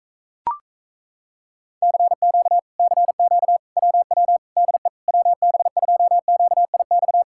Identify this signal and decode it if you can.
Morse code, heard as a keyed tone, about 38 wpm, 700 Hz. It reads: CQ CQ WW DE WB2QIX